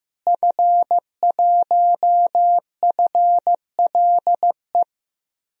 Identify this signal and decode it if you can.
Morse code, heard as a keyed tone, about 15 wpm, 700 Hz. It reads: F1FLE
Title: DXCC-100 Call Signs - 5 Characters - Repeated 2x - 15wpm